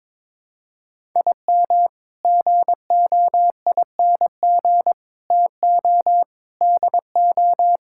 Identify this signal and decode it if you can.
Morse code, heard as a keyed tone, about 22 wpm, 700 Hz. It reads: IM GOING TO DO
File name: practice / 22wpm / Sets of 8 Words - 22wpm